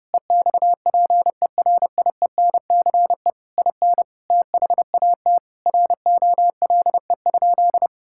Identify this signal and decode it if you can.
Morse code, heard as a keyed tone, about 28 wpm, 700 Hz. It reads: EXPERIENCE IN THAT ROLE?
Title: Sentences from Top 1000 Words - Review Part 6 - 30wpm